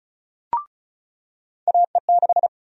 Morse code, heard as a keyed tone, about 36 wpm, 700 Hz. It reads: AE6
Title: USA Call Sign Prefixes - 35wpm